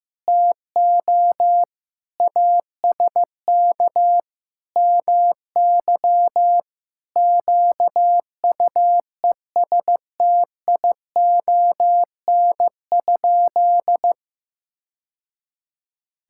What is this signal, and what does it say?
Morse code, heard as a keyed tone, about 15 wpm, 700 Hz. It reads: TO ASK MY QUESTION?
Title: Sentences from Top 1000 Words - Review Part 5 - 15wpm